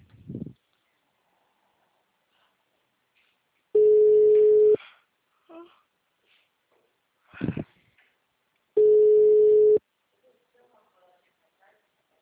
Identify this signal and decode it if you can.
Ringback tone (Europe, 425 Hz), with speech.